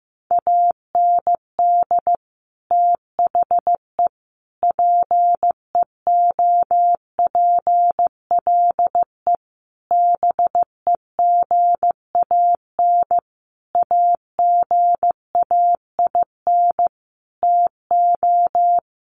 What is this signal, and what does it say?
Morse code, heard as a keyed tone, about 15 wpm, 700 Hz. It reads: AND THE PEOPLE BEGAN AGAIN TO
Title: Sentences from Top 1500 Words 15wpm